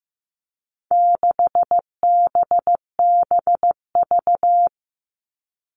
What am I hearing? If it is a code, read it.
Morse code, heard as a keyed tone, about 15 wpm, 700 Hz. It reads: 6BBV
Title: All Cut Numbers - 4-Digit 15wpm